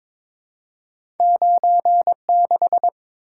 Morse code, heard as a keyed tone, about 22 wpm, 700 Hz. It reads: 96
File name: practice / 22wpm / Two Letter-Number Combinations - 22wpm